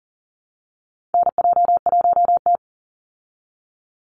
Morse code, heard as a keyed tone, about 40 wpm, 700 Hz. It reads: NJ1T
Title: Call Signs Repeated 3x - 40wpm